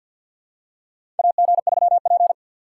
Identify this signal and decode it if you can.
Morse code, heard as a keyed tone, about 48 wpm, 700 Hz. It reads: AG3P
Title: USA Call Signs - 2x Character Spacing - 50wpm